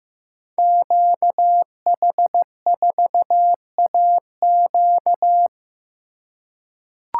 Morse code, heard as a keyed tone, about 15 wpm, 700 Hz. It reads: QH4AQ
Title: Code Groups - 5 Letter:Numbers - 15wpm